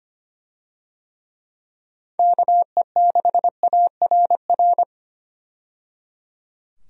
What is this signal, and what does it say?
Morse code, heard as a keyed tone, about 25 wpm, 700 Hz. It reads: KE6ARR
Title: USA Call Signs - 6 Characters - 25wpm